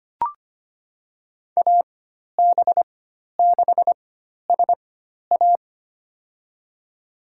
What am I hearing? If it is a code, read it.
Morse code, heard as a keyed tone, about 25 wpm, spaced as for 12 wpm, 700 Hz. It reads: AB6SA
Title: USA Call Signs - 5 Characters - 4x Character Spacing - 25wpm